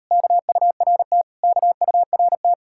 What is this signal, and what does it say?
Morse code, heard as a keyed tone, about 38 wpm, 700 Hz. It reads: KURT KURT